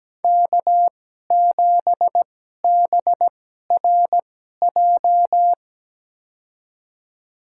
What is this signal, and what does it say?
Morse code, heard as a keyed tone, about 17 wpm, spaced as for 12 wpm, 700 Hz. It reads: K7BRJ
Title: USA Call Signs - 5 Characters - 2x Character Spacing - 17wpm